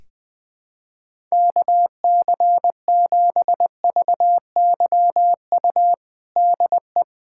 Morse code, heard as a keyed tone, about 20 wpm, 700 Hz. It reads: KC7VYU DE
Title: USA Callsign DE Callsign 20wpm